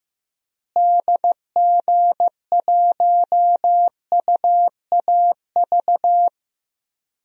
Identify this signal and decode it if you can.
Morse code, heard as a keyed tone, about 15 wpm, 700 Hz. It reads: DG1UAV
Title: DXCC-100 Call Signs - 6 Characters - Repeated 2x - 15wpm